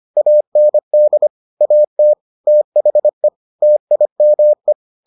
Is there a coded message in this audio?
Yes: Morse code from 0.2 to 4.7 s, heard as a keyed tone, about 25 wpm, 600 Hz. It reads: AND AT THE TIME